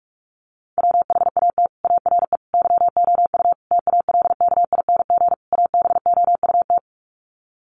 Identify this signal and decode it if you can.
Morse code, heard as a keyed tone, about 44 wpm, 700 Hz. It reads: WHAT ARE YOU TALKING ABOUT
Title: Sets of 5 Words 45wpm